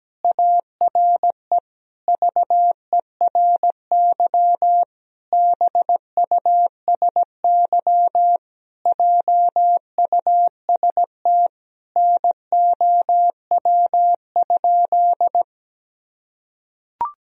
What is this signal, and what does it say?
Morse code, heard as a keyed tone, about 17 wpm, 700 Hz. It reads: ARE VERY BUSY JUST NOW?
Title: Sentences from Top 1000 Words - Review Part 3 - 17wpm